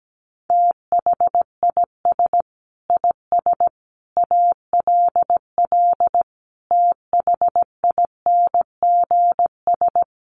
Morse code, heard as a keyed tone, about 17 wpm, 700 Hz. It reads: THIS IS ALL THINGS
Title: Sets of 5 Words 17wpm